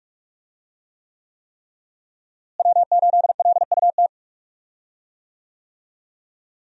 Morse code, heard as a keyed tone, about 46 wpm, 700 Hz. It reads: W8LUT
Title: USA Call Signs - 5 Characters - Repeated 3x - 45wpm